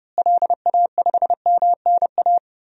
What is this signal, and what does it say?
Morse code, heard as a keyed tone, about 30 wpm, 700 Hz. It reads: LA5MNA
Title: DXCC-100 Call Signs - 6 Characters - Repeated 2x - 30wpm